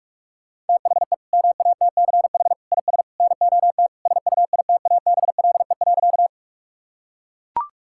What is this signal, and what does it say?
Morse code, heard as a keyed tone, about 41 wpm, 700 Hz. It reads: THE MATCH IS NOT SUITABLE.
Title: Sentences from Top 1600 Words 45wpm